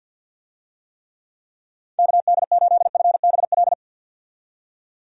Morse code, heard as a keyed tone, about 51 wpm, 700 Hz. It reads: KD8FBL
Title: USA Call Signs - 6 Characters - 50wpm